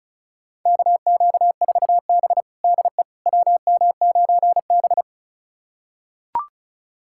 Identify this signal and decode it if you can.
Morse code, heard as a keyed tone, about 36 wpm, 700 Hz. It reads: KQ4B DE WM9B
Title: USA Callsign DE Callsign - 4 Characters - 35wpm